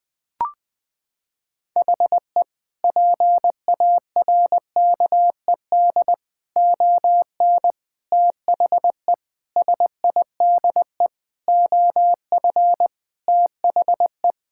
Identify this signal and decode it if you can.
Morse code, heard as a keyed tone, about 20 wpm, 700 Hz. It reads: HE PARKED ON THE SIDE OF THE
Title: Sentences with 5-Word Phrases - 20wpm